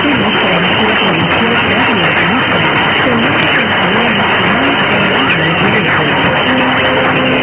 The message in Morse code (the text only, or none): none